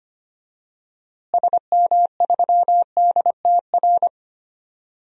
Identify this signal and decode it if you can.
Morse code, heard as a keyed tone, about 25 wpm, 700 Hz. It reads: SM3DTR
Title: DXCC-100 Call Signs - 6 Characters - Repeated 3x - 25wpm